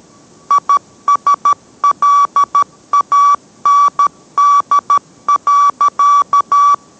Morse code, heard as a keyed tone, about 13 wpm, 1150 Hz. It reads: ISLAND.